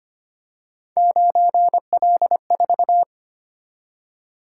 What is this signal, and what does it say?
Morse code, heard as a keyed tone, about 25 wpm, 700 Hz. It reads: 9L4
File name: Three Letter-Number Combinations - 25wpm